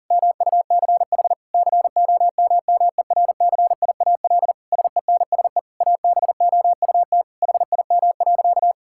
Morse code, heard as a keyed tone, about 40 wpm, 700 Hz. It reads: MUCH COMMERCIAL SENSE ABOUT HIM.
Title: Sentences from Top 1000 Words - Review Part 5 - 40wpm